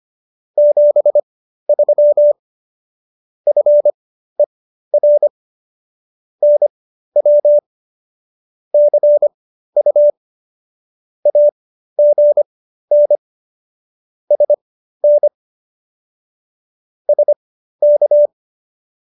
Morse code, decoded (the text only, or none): 73 FER NW CU AGN SN SK